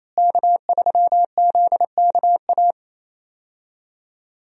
Morse code, heard as a keyed tone, about 27 wpm, 700 Hz. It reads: K3ZKA
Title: USA Call Signs - 5 Characters - Repeated 2x - 28wpm